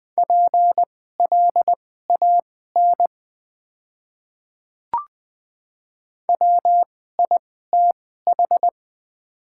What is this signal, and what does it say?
Morse code, heard as a keyed tone, about 20 wpm, spaced as for 15 wpm, 700 Hz. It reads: PLAN WITH
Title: 2x Character Spacing - Top 500 Words - 20wpm